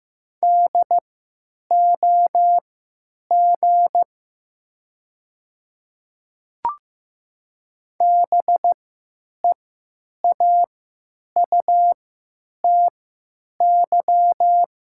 Morse code, heard as a keyed tone, about 15 wpm, spaced as for 9 wpm, 700 Hz. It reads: DOG BEAUTY
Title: 3x Character Spacing - Top 500 Words - 15wpm